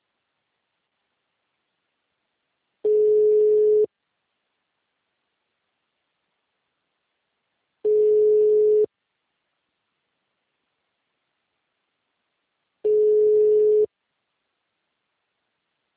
Ringback tone (Europe, 425 Hz).
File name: out- -2010- - - .24